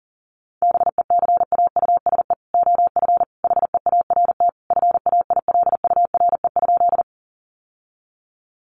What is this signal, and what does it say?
Morse code, heard as a keyed tone, about 40 wpm, 700 Hz. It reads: BECAUSE OF HEART FAILURE?
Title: Sentences from Top 1200 Words 40wpm